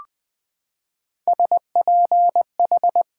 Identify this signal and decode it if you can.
Morse code, heard as a keyed tone, about 20 wpm, 700 Hz. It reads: SPH
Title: Pangram Challenge v2 - 20wpm